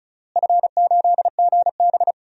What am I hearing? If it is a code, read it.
Morse code, heard as a keyed tone, about 36 wpm, 700 Hz. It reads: F8GB
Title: DXCC-100 Call Signs - 4 Characters - Repeated 3x - 35wpm